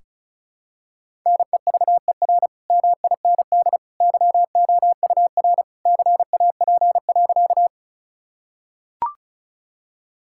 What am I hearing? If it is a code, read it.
Morse code, heard as a keyed tone, about 36 wpm, 700 Hz. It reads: NEVER MIND YOUR CAP.
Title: Sentences from Top 1600 Words 35wpm